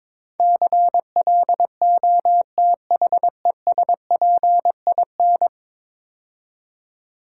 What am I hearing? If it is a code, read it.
Morse code, heard as a keyed tone, about 22 wpm, 700 Hz. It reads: CLOTHESPIN
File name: Easy Words - Speed-Racing with Hints - 22wpm